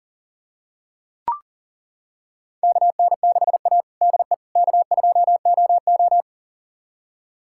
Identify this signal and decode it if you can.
Morse code, heard as a keyed tone, about 40 wpm, 700 Hz. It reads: KN6A DE K2OO
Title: USA Callsign DE Callsign - 4 Characters - 40wpm